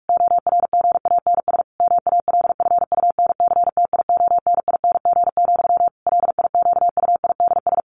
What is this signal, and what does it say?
Morse code, heard as a keyed tone, about 44 wpm, 700 Hz. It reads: ORGANS MALFUNCTIONING, LIQUIDS
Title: A Sound Of Thunder 45wpm